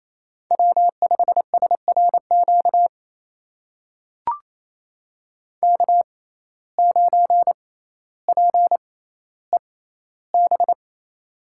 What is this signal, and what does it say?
Morse code, heard as a keyed tone, about 27 wpm, spaced as for 14 wpm, 700 Hz. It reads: W5SRQ K9PEB
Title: USA Call Signs - 5 Characters - 6x Character Spacing - 28wpm